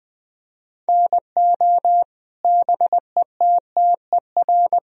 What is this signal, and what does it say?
Morse code, heard as a keyed tone, about 20 wpm, 700 Hz. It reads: NO BETTER
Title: Sentences from Top 1000 Words - Review Part 4 - 20wpm